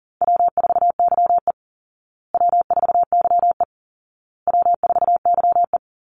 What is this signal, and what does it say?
Morse code, heard as a keyed tone, about 40 wpm, 700 Hz. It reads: W4YE W4YE W4YE
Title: Call Signs Repeated 2x - 40wpm